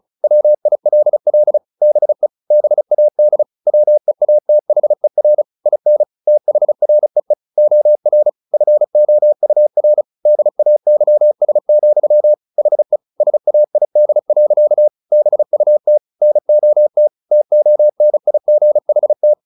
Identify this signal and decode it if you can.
Morse code, heard as a keyed tone, about 35 wpm, 600 Hz. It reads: WILL BE BAD WEATHER IN THREE OR FOUR DAYS, HE SAID. BUT NOT TONIGHT
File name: the old man and the sea 2 35wpm